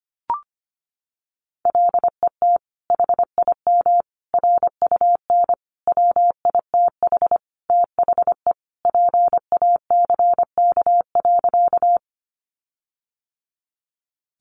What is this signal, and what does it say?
Morse code, heard as a keyed tone, about 25 wpm, 700 Hz. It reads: LET HIM RUN WITH THE PACK.
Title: Sentences from Top 1300 Words 25wpm